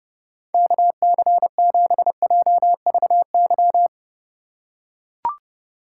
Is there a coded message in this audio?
Yes: Morse code from 0.5 to 3.9 s, heard as a keyed tone, about 29 wpm, 700 Hz. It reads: KC7JVY